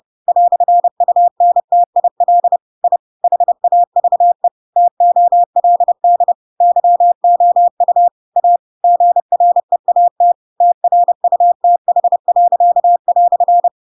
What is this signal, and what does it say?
Morse code, heard as a keyed tone, about 30 wpm, 700 Hz. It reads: "UNTIL I HAVE TOLD YOU A GREAT TRUTH."
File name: Pinocchio / Pinocchio ch4 30 30 ik1hge